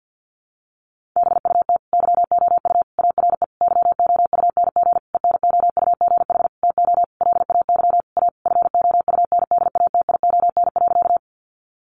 Morse code, heard as a keyed tone, about 50 wpm, 700 Hz. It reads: BUT YOU ARE YOUNG ENOUGH TO LAY A FOUNDATION.
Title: Sentences from Top 1200 Words 50wpm